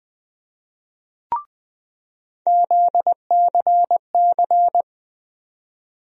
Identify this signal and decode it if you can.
Morse code, heard as a keyed tone, about 20 wpm, 700 Hz. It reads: ZCC
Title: USA Call Sign Postfixes - 3 Letters 20wpm